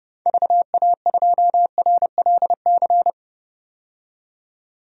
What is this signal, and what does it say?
Morse code, heard as a keyed tone, about 30 wpm, 700 Hz. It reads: VA2RLC
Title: DXCC-100 Call Signs - 6 Characters - 30wpm